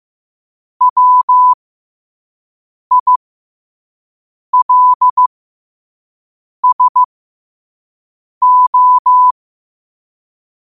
Morse code, heard as a keyed tone, about 15 wpm, spaced as for 5 wpm, 1000 Hz. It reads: WILSO
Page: Quotes for Tue, 12 Aug 2025 in Morse Code at 5 words per minute.